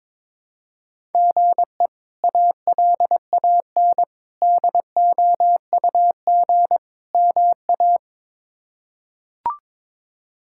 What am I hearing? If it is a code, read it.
Morse code, heard as a keyed tone, about 22 wpm, 700 Hz. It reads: GE ALAN DOUG MA
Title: K1USN SST QSO Practice - 22wpm